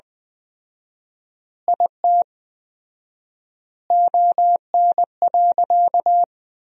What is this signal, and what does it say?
Morse code, heard as a keyed tone, about 20 wpm, 700 Hz. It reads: IT ON.
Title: 4x Word-Spacing - Sentences from Top 200 Words - 20wpm